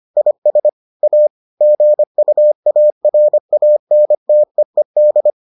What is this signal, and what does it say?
Morse code, heard as a keyed tone, about 25 wpm, 600 Hz. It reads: IS A GUARANTEED